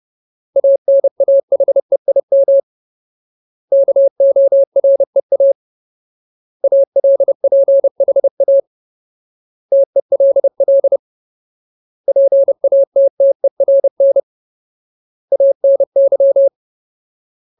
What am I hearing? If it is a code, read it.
Morse code, heard as a keyed tone, about 30 wpm, 550 Hz. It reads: ANAHEIM KOREA ALPHA TELL PATTERN ANY
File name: Ultimate Word List / 30WPM Ultimate